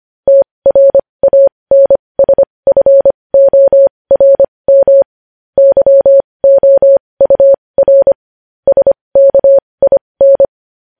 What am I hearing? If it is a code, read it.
Morse code, heard as a keyed tone, about 25 wpm, spaced as for 20 wpm, 550 Hz. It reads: TRANSFORM YOUR SKIN
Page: news-headlines-in-morse-code-at-20-wpm